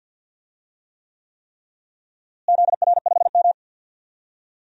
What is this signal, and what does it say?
Morse code, heard as a keyed tone, about 52 wpm, 700 Hz. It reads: ZR5M
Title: DXCC-100 Call Signs - 4 Characters - Repeated 3x - 50wpm